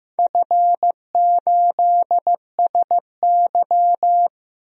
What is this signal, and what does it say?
Morse code, heard as a keyed tone, about 16 wpm, 700 Hz. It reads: F8SY
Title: DXCC-100 Call Signs - 4 Characters - 15wpm